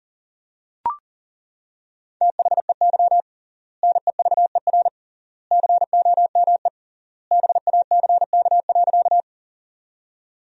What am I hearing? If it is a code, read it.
Morse code, heard as a keyed tone, about 40 wpm, 700 Hz. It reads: THEY NEVER COME BACK.
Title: 3x Word-Spacing - Sentences from Top 300 Words - 40wpm